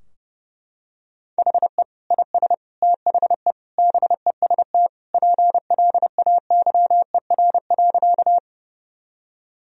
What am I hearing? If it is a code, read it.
Morse code, heard as a keyed tone, about 30 wpm, 700 Hz. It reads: HE IS THE BEST PLAYER.
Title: Sentences from Top 1000 Words - Review Part 4 - 30wpm